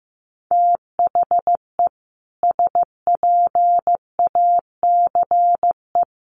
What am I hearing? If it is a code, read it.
Morse code, heard as a keyed tone, about 15 wpm, 700 Hz. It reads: THE SPACE
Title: Sentences from Top 1500 Words 15wpm